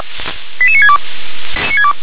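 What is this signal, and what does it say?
Five-tone selective call (ZVEI), heard as quick runs of five tones, tones of about 70 ms. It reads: 88962
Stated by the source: ZVEI five-tone selective call 88705, received by SDR.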